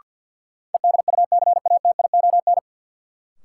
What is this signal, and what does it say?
Morse code, heard as a keyed tone, about 49 wpm, 700 Hz. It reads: EDUCATION
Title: Sentences from Top 1100 Words - New Word Drill Alt - 50wpm